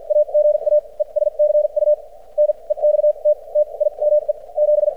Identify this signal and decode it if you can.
Morse code, heard as a keyed tone, about 51 wpm, spaced as for 38 wpm, 600 Hz. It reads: AGAE SCU NEXT TIME 7
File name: archived / CRSnet